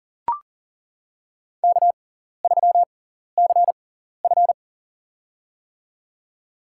Morse code, heard as a keyed tone, about 40 wpm, spaced as for 14 wpm, 700 Hz. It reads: K3CF
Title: USA Call Signs - 6x Character Spacing - 40wpm